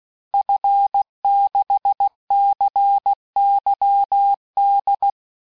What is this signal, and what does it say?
Morse code, heard as a keyed tone, about 16 wpm, 800 Hz. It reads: F6CYD
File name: cours-du-lundi files / cours-1580-fin-QSO